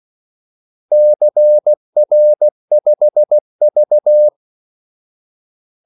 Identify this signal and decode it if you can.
Morse code, heard as a keyed tone, about 16 wpm, 600 Hz. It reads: CR5V
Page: Kaldesignaler 16 wpm